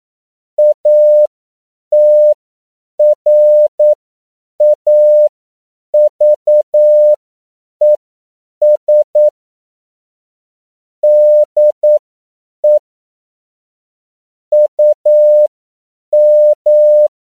Morse code, heard as a keyed tone, about 9 wpm, spaced as for 7 wpm, 600 Hz. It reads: ATRAVES DE UM
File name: CW / 10PPM